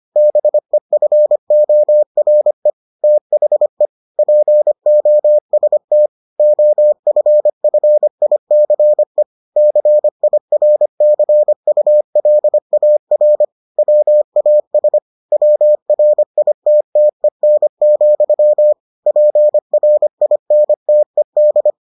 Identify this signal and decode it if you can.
Morse code, heard as a keyed tone, about 25 wpm, 600 Hz. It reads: BEFORE THE POST OFFICE CIRCULAR WAS WRITTEN, PRINTED